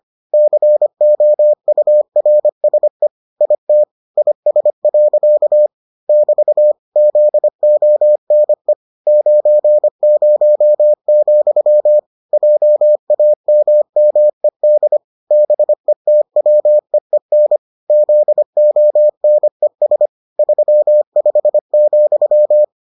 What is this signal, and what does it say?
Morse code, heard as a keyed tone, about 25 wpm, 600 Hz. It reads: COURSE IT IS. <BT> ZONE 90, JAMMED BETWEEN ZONES 35,